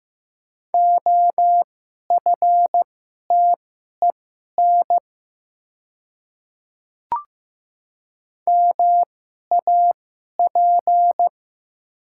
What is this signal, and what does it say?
Morse code, heard as a keyed tone, about 15 wpm, spaced as for 11 wpm, 700 Hz. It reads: OFTEN MAP
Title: 2x Character Spacing - Top 500 Words - 15wpm